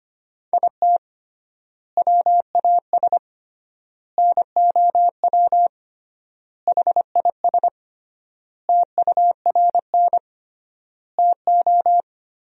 Morse code, heard as a keyed tone, about 25 wpm, 700 Hz. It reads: IT WAS NOW HIS TURN TO
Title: 3x Word-Spacing - Sentences from Top 200 Words - 25wpm